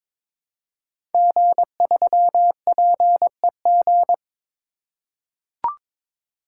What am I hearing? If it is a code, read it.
Morse code, heard as a keyed tone, about 22 wpm, 700 Hz. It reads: G3PEG
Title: Code Groups - 5 Letter:Numbers - 22wpm